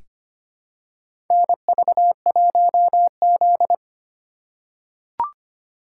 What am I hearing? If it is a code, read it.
Morse code, heard as a keyed tone, about 25 wpm, 700 Hz. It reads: NV1Z
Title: USA Call Signs - 4 Characters - 3x Character Spacing - 25wpm